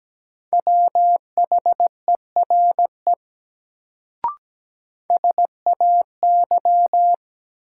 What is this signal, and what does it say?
Morse code, heard as a keyed tone, about 17 wpm, 700 Hz. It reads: WHERE SAY
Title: Sets of 2 Words - New Word Drill - 17wpm